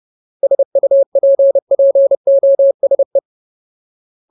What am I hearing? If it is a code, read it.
Morse code, heard as a keyed tone, about 30 wpm, 550 Hz. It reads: SUPPOSE
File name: Ultimate Word List / 30WPM Ultimate